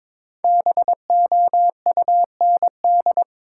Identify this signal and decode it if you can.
Morse code, heard as a keyed tone, about 22 wpm, 700 Hz. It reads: BOUND
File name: 5 Letter Words - First 3 Letters Given - 22wpm